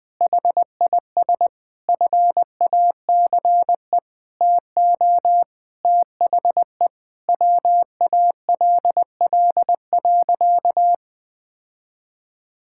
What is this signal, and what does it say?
Morse code, heard as a keyed tone, about 20 wpm, 700 Hz. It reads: HIS FACE TO THE WALL.
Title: Sentences from Top 1000 Words - Review Part 6 - 20wpm